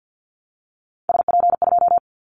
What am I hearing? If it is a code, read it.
Morse code, heard as a keyed tone, about 50 wpm, 700 Hz. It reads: SP2
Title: Callsign Prefixes for Top 25 DXCC - 50wpm